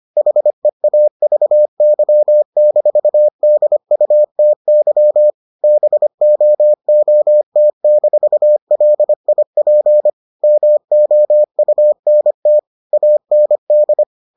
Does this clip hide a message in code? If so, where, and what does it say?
Yes: Morse code from 0.2 to 14.0 s, heard as a keyed tone, about 25 wpm, 600 Hz. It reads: HEAVY-DUTY BOOT-LIP MOUNT AND